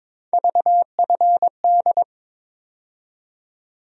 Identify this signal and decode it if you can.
Morse code, heard as a keyed tone, about 22 wpm, 700 Hz. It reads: VFD
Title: USA Call Sign Postfixes - 3 Letters 22wpm